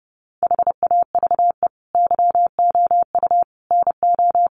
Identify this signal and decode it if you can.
Morse code, heard as a keyed tone, about 30 wpm, 700 Hz. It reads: HAVE YOU NO
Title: Sentences from Top 1500 Words 30wpm